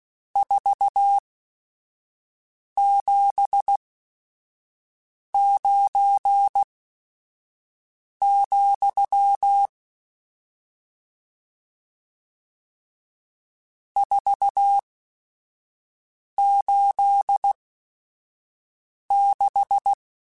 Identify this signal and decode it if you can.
Morse code, heard as a keyed tone, about 16 wpm, spaced as for 5 wpm, 800 Hz. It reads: 479, 486
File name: RSTs / RST-mp3s / Various RST @5